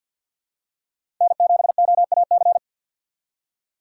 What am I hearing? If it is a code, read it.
Morse code, heard as a keyed tone, about 53 wpm, 700 Hz. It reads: N7OAC
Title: USA Call Signs - 5 Characters - 6x Character Spacing - 50wpm